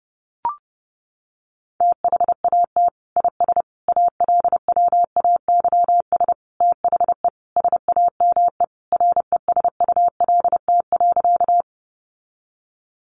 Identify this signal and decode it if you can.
Morse code, heard as a keyed tone, about 28 wpm, 700 Hz. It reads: THAT IS ALWAYS THE SAME RESULT.
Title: Sentences from Top 700 Words 30wpm